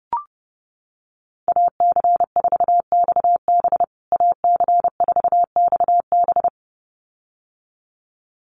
Morse code, heard as a keyed tone, about 30 wpm, 700 Hz. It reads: AC4XB AC4XB
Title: USA Call Signs Repeated 2x 30wpm